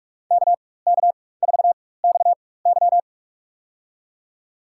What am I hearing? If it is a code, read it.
Morse code, heard as a keyed tone, about 44 wpm, spaced as for 21 wpm, 690 Hz. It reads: KK4XY
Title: USA Call Signs - 5 Characters - 4x Character Spacing - 45wpm